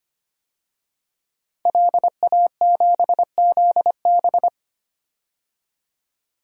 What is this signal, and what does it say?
Morse code, heard as a keyed tone, about 25 wpm, 700 Hz. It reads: LA7ZB
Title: DXCC-100 Call Signs - 5 Characters - 25wpm